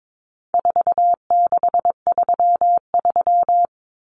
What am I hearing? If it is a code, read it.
Morse code, heard as a keyed tone, about 22 wpm, 700 Hz. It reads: 4633
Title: Common Cut Numbers - 4-Digit 22wpm